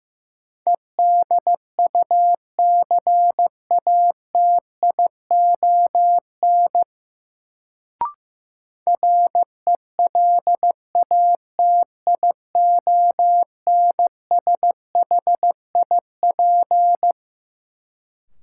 Morse code, heard as a keyed tone, about 16 wpm, 700 Hz. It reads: EDUCATION RELATIONSHIP
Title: Sentences from Top 1100 Words - New Word Drill Alt - 15wpm